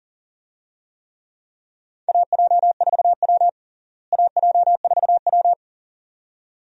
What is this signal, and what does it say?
Morse code, heard as a keyed tone, about 37 wpm, 700 Hz. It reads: AJ4W AJ4W
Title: USA Call Signs - 4 Characters - Repeated 2x - 40wpm